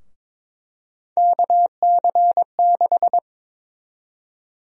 Morse code, heard as a keyed tone, about 22 wpm, 700 Hz. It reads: KC6
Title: USA Call Sign Prefixes - 22wpm